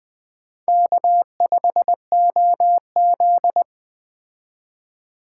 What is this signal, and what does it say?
Morse code, heard as a keyed tone, about 20 wpm, 700 Hz. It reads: K5OZ
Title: Speed Racing - USA 1x2 Call Signs 17wpm